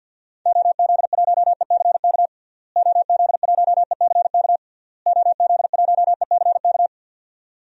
Morse code, heard as a keyed tone, about 49 wpm, 700 Hz. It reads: OZ1ECK OZ1ECK OZ1ECK
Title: DXCC-100 Call Signs - 6 Characters - Repeated 3x - 50wpm